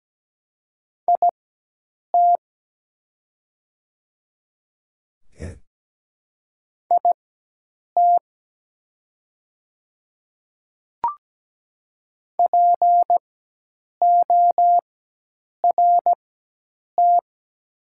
Morse code, heard as a keyed tone, about 17 wpm, spaced as for 8 wpm, 700 Hz. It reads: IT IT PORT
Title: 4x Character Spacing - Top 500 Words - 17wpm